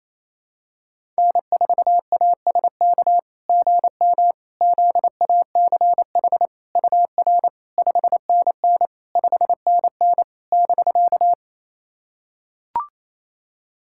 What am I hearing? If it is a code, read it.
Morse code, heard as a keyed tone, about 28 wpm, 690 Hz. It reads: N4ASK GM ZACH UR 5NN 5NN <BK>
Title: SOTA QSO Practice 28wpm